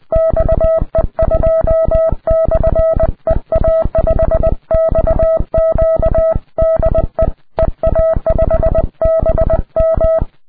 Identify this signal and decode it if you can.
Morse code, heard as a keyed tone, about 20 wpm, 650 Hz. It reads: XE2/EA5XQ DE EA5BM
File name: ea5xq / qso ea5bm xe2 ea5xq qrp